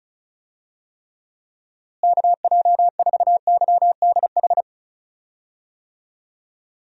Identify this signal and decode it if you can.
Morse code, heard as a keyed tone, about 36 wpm, 700 Hz. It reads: KJ4YDH
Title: USA Call Signs - 6 Characters - Repeated 2x - 35wpm